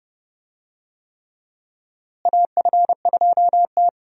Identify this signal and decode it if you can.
Morse code, heard as a keyed tone, about 30 wpm, 700 Hz. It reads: AF2T
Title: USA Call Signs - 4 Characters - Repeated 3x - 30wpm